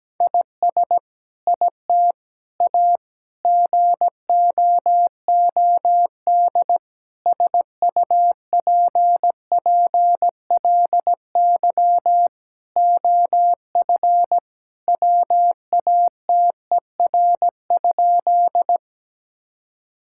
Morse code, heard as a keyed tone, about 17 wpm, 700 Hz. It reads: IS IT A GOOD SUPPLY OF WATER?
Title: Sentences from Top 1000 Words - Review Part 5 - 17wpm